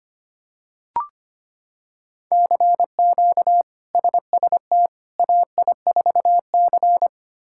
Morse code, heard as a keyed tone, about 25 wpm, 700 Hz. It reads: CQ SST AI4C
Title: K1USN SST QSO Practice - 25wpm